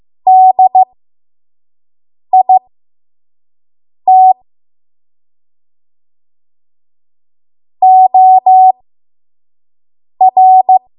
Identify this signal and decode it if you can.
Morse code, heard as a keyed tone, about 15 wpm, spaced as for 5 wpm, 750 Hz. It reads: DIT OR
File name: Morse / 05WPM